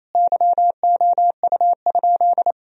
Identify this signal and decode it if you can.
Morse code, heard as a keyed tone, about 28 wpm, 700 Hz. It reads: YOU?
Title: Sentences from Top 1000 Words - Review Part 2 - 28wpm